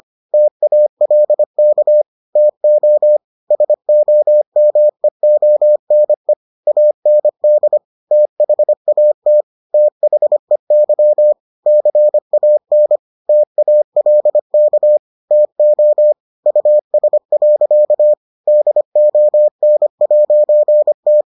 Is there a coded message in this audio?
Yes: Morse code from 0.3 to 21.2 s, heard as a keyed tone, about 25 wpm, 600 Hz. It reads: TALK TO SOMEONE AND THAT THEY CAN TALK TO US. DON'T